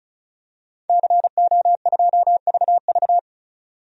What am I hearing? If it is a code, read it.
Morse code, heard as a keyed tone, about 36 wpm, 700 Hz. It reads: CO2VV
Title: DXCC-100 Call Signs - 5 Characters - 35wpm